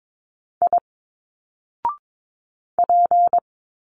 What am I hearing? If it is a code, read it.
Morse code, heard as a keyed tone, about 22 wpm, 700 Hz. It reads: I P
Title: Single Letters 22wpm